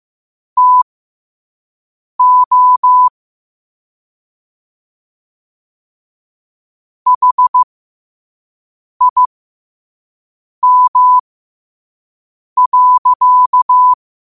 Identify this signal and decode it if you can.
Morse code, heard as a keyed tone, about 15 wpm, spaced as for 5 wpm, 1000 Hz. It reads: TO HIM.
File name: qotdfiles / 5wpm